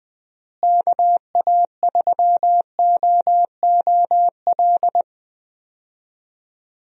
Morse code, heard as a keyed tone, about 20 wpm, 700 Hz. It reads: KA3OOL
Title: USA Call Signs - 6 Characters - Repeated 3x - 20wpm